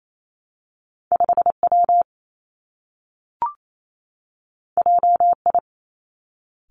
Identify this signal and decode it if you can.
Morse code, heard as a keyed tone, about 28 wpm, 700 Hz. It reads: 5 W J I
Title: Two Challenging Letter-Number Combinations 28wpm